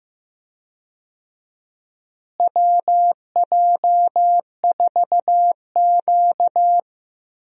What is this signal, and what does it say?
Morse code, heard as a keyed tone, about 16 wpm, 700 Hz. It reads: WJ4Q
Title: USA Call Signs - 4 Characters - Repeated 3x - 15wpm